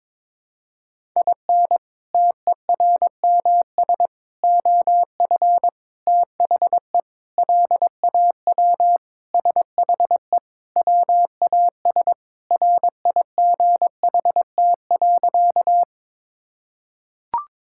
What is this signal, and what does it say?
Morse code, heard as a keyed tone, about 22 wpm, 700 Hz. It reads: IN TERMS OF THE LAW SHE WAS RIGHT.
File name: practice / 22wpm / Sentences with 4-Word Phrases - 22wpm